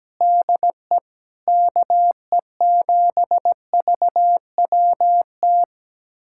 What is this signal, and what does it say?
Morse code, heard as a keyed tone, about 17 wpm, 700 Hz. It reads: DE KE7VWT
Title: USA Callsign DE Callsign - 6 Characters - 17wpm